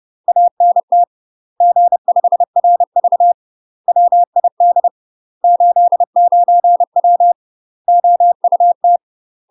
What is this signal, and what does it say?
Morse code, heard as a keyed tone, about 30 wpm, 700 Hz. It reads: ANT G5RV WID 89W OUT
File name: UK MASTER QSOs TRAINING / 30WPM / UK MASTER QSOs 0019